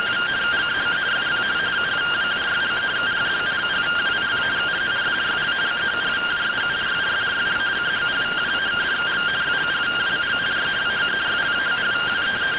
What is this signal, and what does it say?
Two-tone teleprinter signal, about 170 Hz shift.